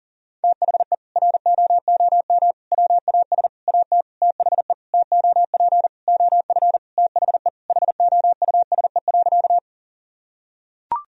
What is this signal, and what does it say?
Morse code, heard as a keyed tone, about 37 wpm, 700 Hz. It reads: THE ROOM WAS AT THE TOP OF THE HOUSE.
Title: Sentences from Top 1000 Words - Review Part 2 - 40wpm